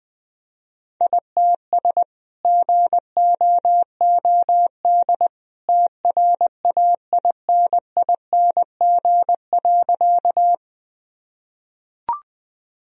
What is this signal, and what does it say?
Morse code, heard as a keyed tone, about 20 wpm, 700 Hz. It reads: ITS GOOD TRAINING.
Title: Sentences from Top 1000 Words - Review Part 4 - 20wpm